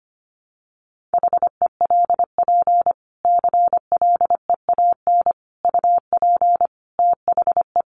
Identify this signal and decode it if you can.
Morse code, heard as a keyed tone, about 25 wpm, 700 Hz. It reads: HELP CLEAN UP THE
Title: Sentences from Top 1100 Words 25wpm